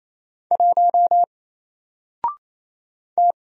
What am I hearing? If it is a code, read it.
Morse code, heard as a keyed tone, about 28 wpm, 700 Hz. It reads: 1 T
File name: Single-Letter-Number-28wpm